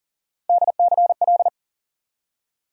Morse code, heard as a keyed tone, about 40 wpm, 700 Hz. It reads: DCL
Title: USA Call Sign Postfixes - 3 Letters 40wpm